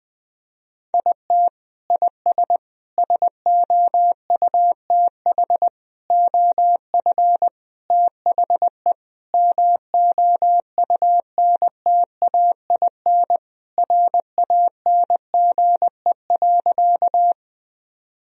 Morse code, heard as a keyed tone, about 20 wpm, 700 Hz. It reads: IT IS SOUTH OF THE MOUNTAIN RANGE.
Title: Sentences from Top 1000 Words - Review Part 4 - 20wpm